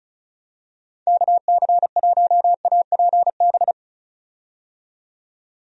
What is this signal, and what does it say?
Morse code, heard as a keyed tone, about 36 wpm, 700 Hz. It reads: KC1APB
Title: USA Call Signs - 6 Characters - 35wpm